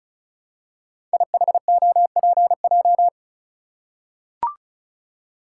Morse code, heard as a keyed tone, about 36 wpm, 690 Hz. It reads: IHOPJ
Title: Code Groups - 5 Letters - 35wpm